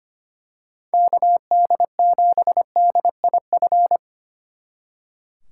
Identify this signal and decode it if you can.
Morse code, heard as a keyed tone, about 29 wpm, 700 Hz. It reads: KD7DIF
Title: Speed Racing - USA 2x3 Call Signs 25wpm